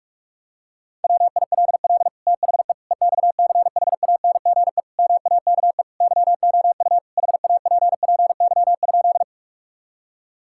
Morse code, heard as a keyed tone, about 41 wpm, 700 Hz. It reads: WILL THE EXCHANGE MAKE YOU HAPPY?
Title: Sentences from Top 1000 Words - Review Part 6 - 45wpm